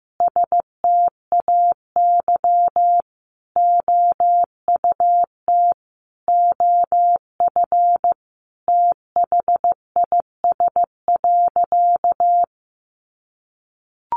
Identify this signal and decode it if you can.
Morse code, heard as a keyed tone, about 15 wpm, 700 Hz. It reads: STAY OUT OF THIS.
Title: A Sound Of Thunder 15wpm